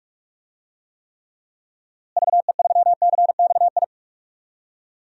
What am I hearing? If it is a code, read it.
Morse code, heard as a keyed tone, about 44 wpm, 700 Hz. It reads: VE3CXI
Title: DXCC-100 Call Signs - 6 Characters - Repeated 3x - 45wpm